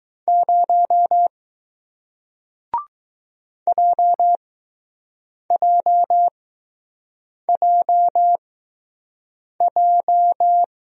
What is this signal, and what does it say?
Morse code, heard as a keyed tone, about 20 wpm, spaced as for 6 wpm, 700 Hz. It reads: 0 JJJJ